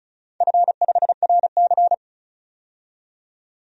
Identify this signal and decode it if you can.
Morse code, heard as a keyed tone, about 36 wpm, 700 Hz. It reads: F5RC
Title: DXCC-100 Call Signs - 4 Characters - 35wpm